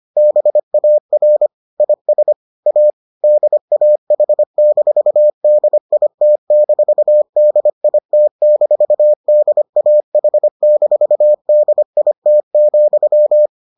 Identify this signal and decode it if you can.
Morse code, heard as a keyed tone, about 25 wpm, 600 Hz. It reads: BAR IS A DAH-DIT-DIT-DAH-DIT,